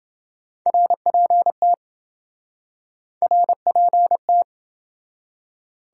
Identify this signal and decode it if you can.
Morse code, heard as a keyed tone, about 27 wpm, 700 Hz. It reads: RPT RPT